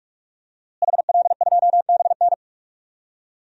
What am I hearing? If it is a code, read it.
Morse code, heard as a keyed tone, about 46 wpm, 700 Hz. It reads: HL2BN
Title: DXCC-100 Call Signs - 5 Characters - Repeated 2x - 45wpm